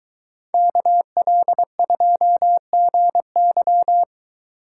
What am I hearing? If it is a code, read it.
Morse code, heard as a keyed tone, about 23 wpm, 700 Hz. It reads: KL2GY